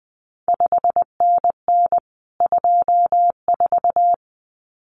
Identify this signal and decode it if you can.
Morse code, heard as a keyed tone, about 20 wpm, 700 Hz. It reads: 5NN 24
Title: CQ WW DX Contest Exchange Encore - 20wpm